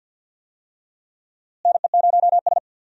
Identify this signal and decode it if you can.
Morse code, heard as a keyed tone, about 48 wpm, 700 Hz. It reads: NE0S
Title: USA Call Signs - 4 Characters - Repeated 2x - 50wpm